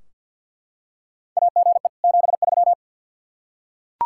Morse code, heard as a keyed tone, about 52 wpm, 700 Hz. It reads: AGE 73
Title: Practice QSO - Exchange 3 - More Info - 50wpm